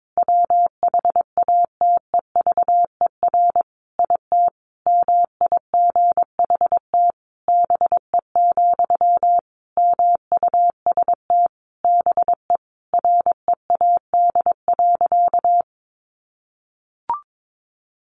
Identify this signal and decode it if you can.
Morse code, heard as a keyed tone, about 22 wpm, 700 Hz. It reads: WHATEVER IT MIGHT BE, MUST BE READ.
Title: Sentences from Top 1100 Words 22wpm